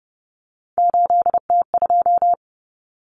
Morse code, heard as a keyed tone, about 30 wpm, 700 Hz. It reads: 8T2
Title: Common Cut Numbers - 3-Digit 30wpm